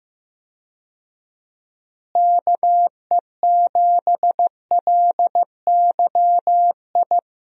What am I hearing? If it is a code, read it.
Morse code, heard as a keyed tone, about 16 wpm, 700 Hz. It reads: KE7LYI
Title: USA Call Signs - 6 Characters - Repeated 2x - 15wpm